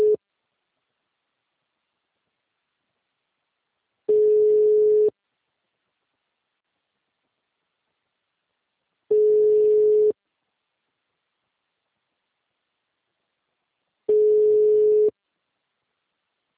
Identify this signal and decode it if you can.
Ringback tone (Europe, 425 Hz).
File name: out- -2002- - - .10903